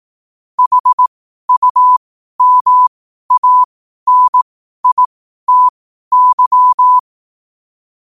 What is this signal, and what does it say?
Morse code, heard as a keyed tone, about 18 wpm, spaced as for 12 wpm, 1000 Hz. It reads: HUMANITY
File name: qotdfiles / 12wpm